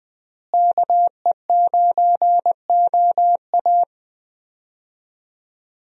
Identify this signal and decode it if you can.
Morse code, heard as a keyed tone, about 21 wpm, 700 Hz. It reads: KE9OA
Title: USA Call Signs - 5 Characters - Repeated 3x - 20wpm